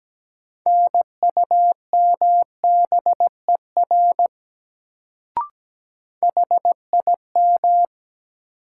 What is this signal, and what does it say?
Morse code, heard as a keyed tone, about 17 wpm, 700 Hz. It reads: NUMBER HIM
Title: Sentences from Top 100 Words - New Word Drill - 17wpm